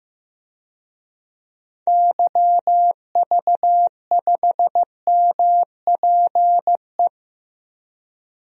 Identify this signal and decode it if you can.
Morse code, heard as a keyed tone, about 15 wpm, 700 Hz. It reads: YV5MPE
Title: DXCC-100 Call Signs - 6 Characters - Repeated 3x - 15wpm